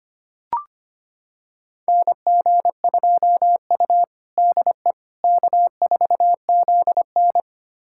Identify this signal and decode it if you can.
Morse code, heard as a keyed tone, about 25 wpm, 700 Hz. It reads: NG2U DE K4ZN
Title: USA Callsign DE Callsign - 4 Characters - 25wpm